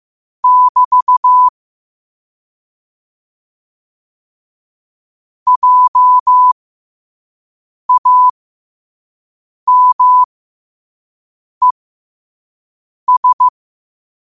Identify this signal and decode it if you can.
Morse code, heard as a keyed tone, about 15 wpm, spaced as for 5 wpm, 1000 Hz. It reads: <BT> JAMES
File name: qotdfiles / 5wpm